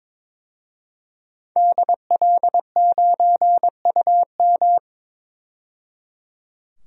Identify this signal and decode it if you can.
Morse code, heard as a keyed tone, about 22 wpm, 700 Hz. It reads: DL9UM
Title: DXCC-100 Call Signs - 5 Characters - 22wpm